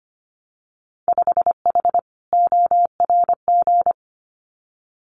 Morse code, heard as a keyed tone, about 25 wpm, 700 Hz. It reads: 5H ORG
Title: Winter Field Day Exchanges Encore 25wpm